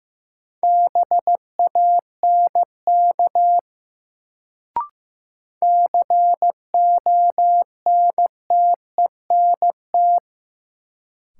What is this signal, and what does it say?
Morse code, heard as a keyed tone, about 15 wpm, 700 Hz. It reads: BANK CONTENT
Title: Sentences from Top 900 Words - New Word Drill - 15wpm